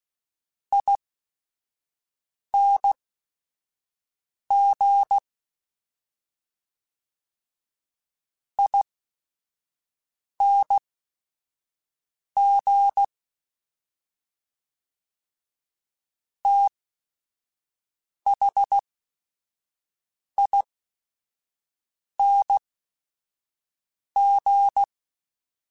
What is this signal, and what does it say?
Morse code, heard as a keyed tone, about 16 wpm, spaced as for 5 wpm, 800 Hz. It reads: ING ING THING